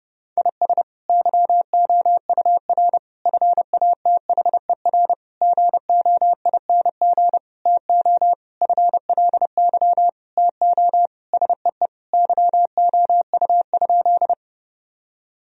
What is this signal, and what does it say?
Morse code, heard as a keyed tone, about 32 wpm, 700 Hz. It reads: IS YOUR FATHER GOING TO FLY TO SEE YOU?
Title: Sentences from Top 1000 Words - Review Part 2 - 30wpm